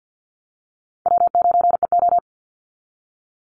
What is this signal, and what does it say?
Morse code, heard as a keyed tone, about 48 wpm, 700 Hz. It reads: W9EO
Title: Easy Call Signs Repeated 2x - 50wpm